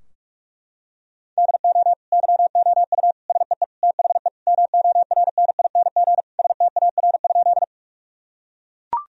Morse code, heard as a keyed tone, about 42 wpm, 700 Hz. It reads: DO YOU SEE THE MORNING STAR?
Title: Sentences from Top 1000 Words - Review Part 2 - 45wpm